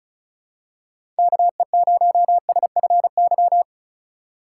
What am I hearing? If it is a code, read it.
Morse code, heard as a keyed tone, about 34 wpm, 700 Hz. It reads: KE0SFY
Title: USA Call Signs - 6 Characters - 4x Character Spacing - 35wpm